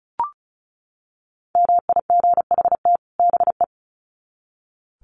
Morse code, heard as a keyed tone, about 34 wpm, 700 Hz. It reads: MIGHT BE
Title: Sets of 2 Words 35wpm